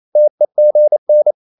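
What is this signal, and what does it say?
Morse code, heard as a keyed tone, about 28 wpm, 600 Hz. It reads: TEGN
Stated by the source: Danish short words sent in Morse at 28 wpm, from CW med Gnister.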